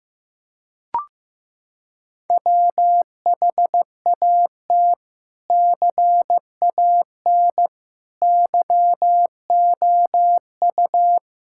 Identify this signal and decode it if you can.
Morse code, heard as a keyed tone, about 15 wpm, 700 Hz. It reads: WHAT CAN YOU
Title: Sentences from Top 1000 Words - Review Part 6 - 15wpm